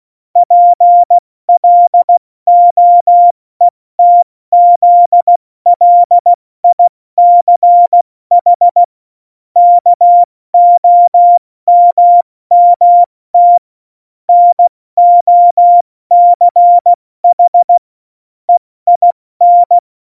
Morse code, heard as a keyed tone, about 16 wpm, spaced as for 14 wpm, 700 Hz. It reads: PLOETZLICH KOMMT NOCH EIN
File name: Morsekurs / Witze / Witz 25. September 2018, 70 F80